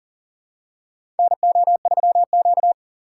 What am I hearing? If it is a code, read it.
Morse code, heard as a keyed tone, about 40 wpm, 700 Hz. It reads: NO3Q
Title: USA Call Signs - 4 Characters - 6x Character Spacing - 40wpm